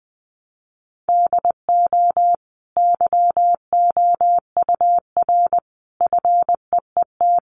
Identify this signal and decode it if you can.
Morse code, heard as a keyed tone, about 20 wpm, 700 Hz. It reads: DO YOUR FEET
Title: Sentences from Top 800 Words 20wpm